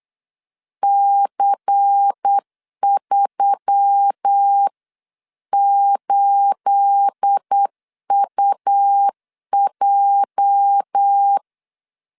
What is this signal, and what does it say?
Morse code, heard as a keyed tone, about 8 wpm, 800 Hz. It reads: C38UJ
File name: CW / DET / 1 9 1500C S B 8wpm 5624